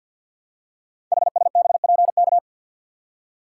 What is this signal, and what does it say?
Morse code, heard as a keyed tone, about 48 wpm, 700 Hz. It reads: HSBPK
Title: Code Groups - 5 Letters - 50wpm